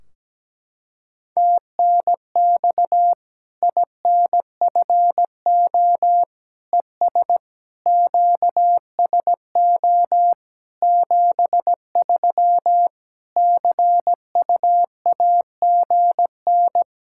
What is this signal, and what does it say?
Morse code, heard as a keyed tone, about 17 wpm, 700 Hz. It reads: TNX INFO ES QSO 73 CUAGN
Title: Practice QSO - Exchange 4 - Ending - 17wpm